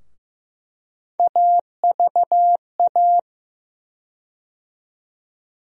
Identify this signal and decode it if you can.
Morse code, heard as a keyed tone, about 15 wpm, 700 Hz. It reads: AVA
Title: USA Call Sign Postfixes - 3 Letters 15wpm